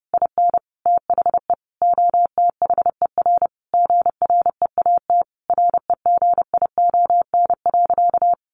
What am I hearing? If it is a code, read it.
Morse code, heard as a keyed tone, about 30 wpm, 700 Hz. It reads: IN THE OTHER GREAT REGION.
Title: Sentences from Top 1500 Words 30wpm